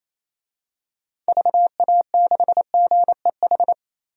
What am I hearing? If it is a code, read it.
Morse code, heard as a keyed tone, about 28 wpm, 700 Hz. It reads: VA6GEH
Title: DXCC-100 Call Signs - 6 Characters - Repeated 3x - 28wpm